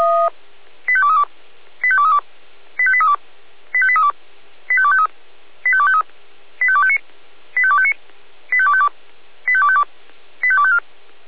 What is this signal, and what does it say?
Five-tone selective call (ZVEI), heard as quick runs of five tones, tones of about 70 ms. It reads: 85232 85232 85832 85832 85253 85253 85259 85259 85252 85252 85254
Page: SDR/ZVEI